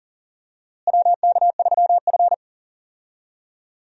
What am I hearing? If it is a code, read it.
Morse code, heard as a keyed tone, about 39 wpm, 700 Hz. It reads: WK3F
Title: USA Call Signs - 4 Characters - Repeated 2x - 40wpm